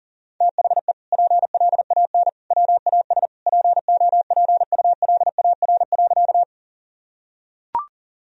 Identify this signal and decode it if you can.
Morse code, heard as a keyed tone, about 40 wpm, 700 Hz. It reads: THE PLAN WAS POPULAR.
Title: Sentences from Top 1000 Words - Review Part 3 - 40wpm